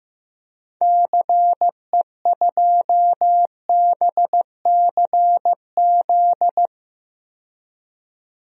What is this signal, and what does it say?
Morse code, heard as a keyed tone, about 15 wpm, 700 Hz. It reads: CE2BCZ
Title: DXCC-100 Call Signs - 6 Characters - 15wpm